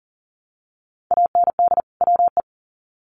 Morse code, heard as a keyed tone, about 40 wpm, 700 Hz. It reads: AND WE
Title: Sets of 2 Words 40wpm